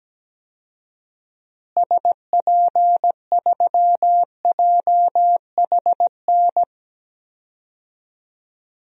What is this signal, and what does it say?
Morse code, heard as a keyed tone, about 17 wpm, 700 Hz. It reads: SP3JHN
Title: DXCC-100 Call Signs - 6 Characters - Repeated 2x - 17wpm